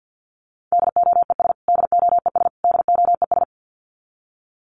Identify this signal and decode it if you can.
Morse code, heard as a keyed tone, about 49 wpm, 700 Hz. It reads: DOES DOES DOES
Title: Top 300 Words Repeated 50wpm